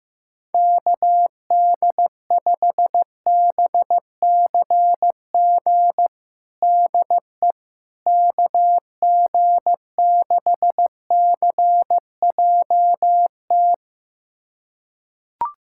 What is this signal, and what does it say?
Morse code, heard as a keyed tone, about 15 wpm, 700 Hz. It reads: KD5BCG DE KG6CJT
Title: USA Callsign DE Callsign 15wpm